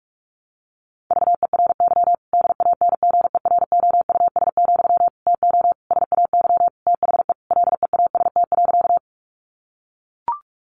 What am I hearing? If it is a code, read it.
Morse code, heard as a keyed tone, about 44 wpm, 700 Hz. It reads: VERY DANGEROUS, TO SAY THE LEAST.
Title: Sentences from Top 1200 Words 45wpm